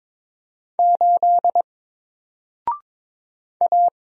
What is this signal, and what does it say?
Morse code, heard as a keyed tone, about 22 wpm, 700 Hz. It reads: 8 A
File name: Single-Letter-Number-22wpm